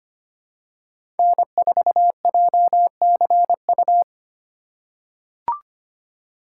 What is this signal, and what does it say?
Morse code, heard as a keyed tone, about 25 wpm, 700 Hz. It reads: N4JCU
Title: USA Call Signs - 4x Character Spacing - 25wpm